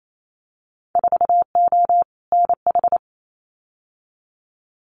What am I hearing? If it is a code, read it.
Morse code, heard as a keyed tone, about 28 wpm, 700 Hz. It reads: 4O NH
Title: Winter Field Day Exchanges Encore 28wpm